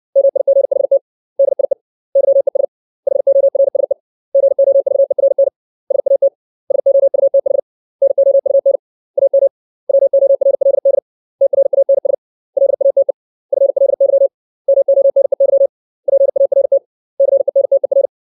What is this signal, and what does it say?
Morse code, heard as a keyed tone, about 60 wpm, 550 Hz. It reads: MIGHT DIE YES HORSE GOVERN SAT SOUTH NOUN AM WORLD TRANS LATE FLY MONEY PART CENTER
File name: 632 words / 632 Words 60WPM